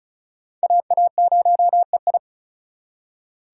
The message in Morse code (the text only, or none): AA0EI